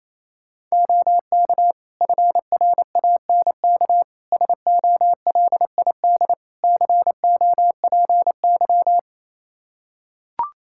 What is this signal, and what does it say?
Morse code, heard as a keyed tone, about 28 wpm, 700 Hz. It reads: OK FRANK SOLID COPY
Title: Practice QSO - Exchange 3 - More Info - 28wpm